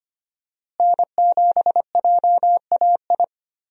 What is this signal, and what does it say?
Morse code, heard as a keyed tone, about 25 wpm, 700 Hz. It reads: N7JAI
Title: USA Call Signs - 5 Characters - Repeated 3x - 25wpm